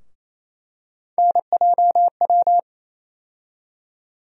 Morse code, heard as a keyed tone, about 28 wpm, 700 Hz. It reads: NJW
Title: USA Call Sign Postfixes - 3 Letters 28wpm